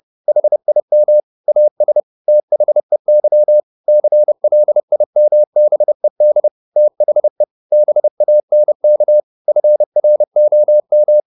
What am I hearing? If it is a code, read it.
Morse code, heard as a keyed tone, about 30 wpm, 600 Hz. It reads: HIM AS THEY CLIMBED THE BANK FROM
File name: the old man and the sea / the old man and the sea 1 30wpm